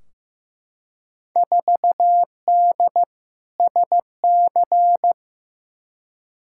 Morse code, heard as a keyed tone, about 15 wpm, 700 Hz. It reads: 4D SC
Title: Field Day Exchanges Encore - 15wpm